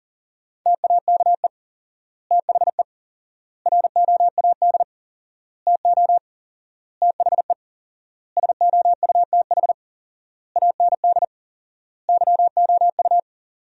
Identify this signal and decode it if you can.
Morse code, heard as a keyed tone, about 37 wpm, 700 Hz. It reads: TAKE THE ROAD TO THE SOUTH AND YOU
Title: 4x Word-Spacing - Sentences from Top 400 Words - 40wpm